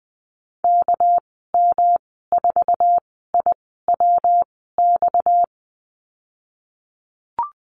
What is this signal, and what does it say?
Morse code, heard as a keyed tone, about 20 wpm, spaced as for 15 wpm, 700 Hz. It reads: KM4IWX
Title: 238 Lesson 38g - Letter q - Call Signs 20wpm